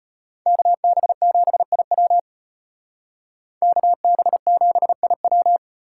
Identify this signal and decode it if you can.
Morse code, heard as a keyed tone, about 32 wpm, 700 Hz. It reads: KB7IW KB7IW